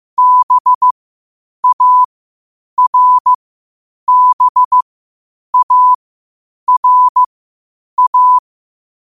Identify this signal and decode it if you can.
Morse code, heard as a keyed tone, about 15 wpm, spaced as for 8 wpm, 1000 Hz. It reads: BARBARA
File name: qotdfiles / 8wpm